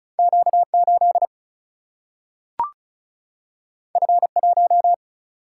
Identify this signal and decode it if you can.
Morse code, heard as a keyed tone, about 36 wpm, 700 Hz. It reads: Q 8 F 1
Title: Single-Letter Single-Number 35wpm